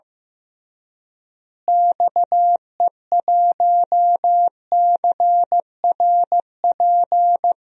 Morse code, heard as a keyed tone, about 15 wpm, 700 Hz. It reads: XE1CRP
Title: DXCC-100 Call Signs - 6 Characters - Repeated 3x - 15wpm